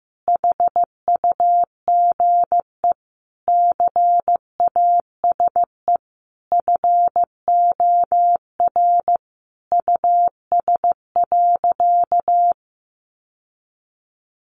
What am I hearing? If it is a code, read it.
Morse code, heard as a keyed tone, about 15 wpm, 700 Hz. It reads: HUGE CASE FOR US.
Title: Sentences from Top 1100 Words 15wpm